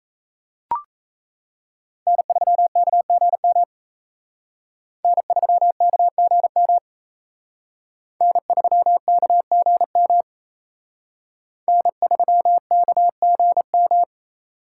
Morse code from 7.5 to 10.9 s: N3KGM

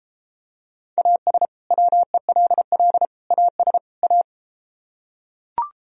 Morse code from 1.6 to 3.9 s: WELL AS

